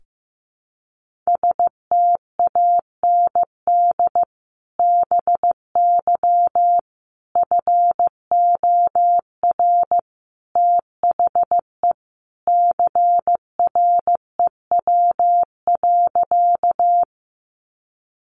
Morse code, heard as a keyed tone, about 15 wpm, 700 Hz. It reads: STAND BY FOR THE CREW.
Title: Sentences from Top 1500 Words 15wpm